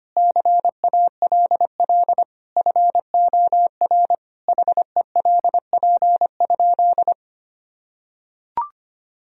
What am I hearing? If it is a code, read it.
Morse code, heard as a keyed tone, about 25 wpm, 700 Hz. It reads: CALL FOR HELP?
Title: Sentences from Top 1000 Words - Review Part 4 - 25wpm